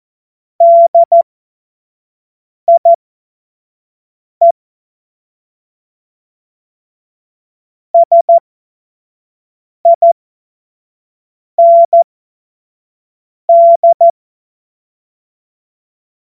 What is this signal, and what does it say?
Morse code, heard as a keyed tone, about 14 wpm, spaced as for 5 wpm, 680 Hz. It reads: DIE SIND
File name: Morsekurs / Witze / Witz 28. September 2021, 25 F70